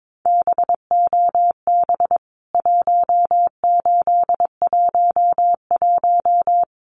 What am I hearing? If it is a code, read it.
Morse code, heard as a keyed tone, about 22 wpm, 700 Hz. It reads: BOB 1811
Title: CWT Exchange 22wpm